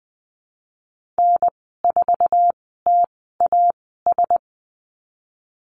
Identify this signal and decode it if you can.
Morse code, heard as a keyed tone, about 20 wpm, spaced as for 15 wpm, 700 Hz. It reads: N4TAS
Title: 048 Lesson 9g - Number 4 - Call Signs 20wpm